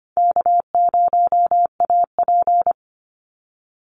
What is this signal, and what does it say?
Morse code, heard as a keyed tone, about 25 wpm, 700 Hz. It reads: K0AP
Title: Easy Call Signs - 25wpm